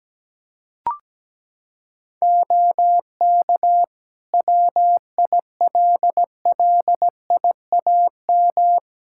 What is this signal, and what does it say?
Morse code, heard as a keyed tone, about 17 wpm, 700 Hz. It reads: OK WILLIAM
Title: Practice QSO - Exchange 4 - Ending - 17wpm